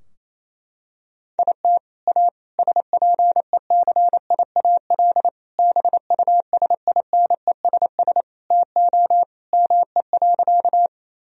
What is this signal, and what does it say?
Morse code, heard as a keyed tone, about 27 wpm, 700 Hz. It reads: IT A SPECIAL BUSINESS TO ME.
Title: Sentences from Top 1000 Words - Review Part 3 - 28wpm